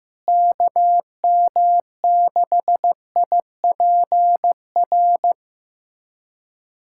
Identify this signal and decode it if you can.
Morse code, heard as a keyed tone, about 15 wpm, 700 Hz. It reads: KM6IPR
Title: USA Callsign DE Callsign - 6 Characters - 15wpm